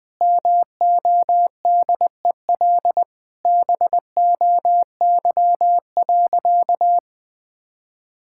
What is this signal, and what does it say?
Morse code, heard as a keyed tone, about 20 wpm, 700 Hz. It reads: MODEL BOY.
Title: Sentences from Top 1000 Words - Review Part 5 - 20wpm